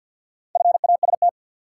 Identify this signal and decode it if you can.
Morse code, heard as a keyed tone, about 53 wpm, 700 Hz. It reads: FAST
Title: Sentences from Top 400 Words - New Word Drill - 50wpm